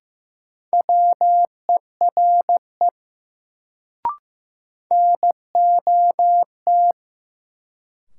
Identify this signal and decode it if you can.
Morse code, heard as a keyed tone, about 15 wpm, 700 Hz. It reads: WERE NOT
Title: Sets of 2 Words - New Word Drill - 15wpm